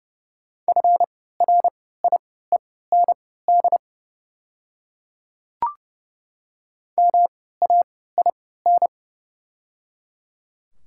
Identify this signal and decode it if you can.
Morse code, heard as a keyed tone, about 30 wpm, spaced as for 17 wpm, 700 Hz. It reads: FRIEND MAIN
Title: 3x Character Spacing - Top 300 Words - 30wpm